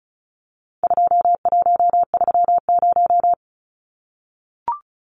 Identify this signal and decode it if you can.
Morse code, heard as a keyed tone, about 34 wpm, 700 Hz. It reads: 2130
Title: Numbers - 4-Digit 35wpm